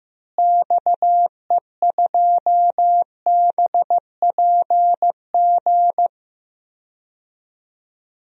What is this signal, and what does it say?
Morse code, heard as a keyed tone, about 15 wpm, 700 Hz. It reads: XE2BPG
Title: DXCC-100 Call Signs - 6 Characters - Repeated 3x - 15wpm